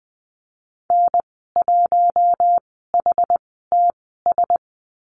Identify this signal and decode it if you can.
Morse code, heard as a keyed tone, about 20 wpm, spaced as for 15 wpm, 700 Hz. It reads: N1HTS
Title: 088 Lesson 15g - Number 5 - Call Signs 20wpm